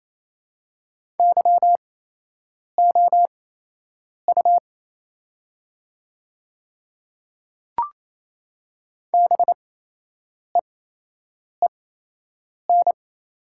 Morse code, heard as a keyed tone, about 28 wpm, spaced as for 8 wpm, 700 Hz. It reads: YOU BEEN
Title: 8x Character Spacing - Top 100 Words - 28wpm